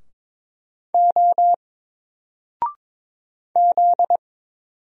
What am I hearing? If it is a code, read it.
Morse code, heard as a keyed tone, about 22 wpm, 700 Hz. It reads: O Z
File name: Single-Letter-Number-22wpm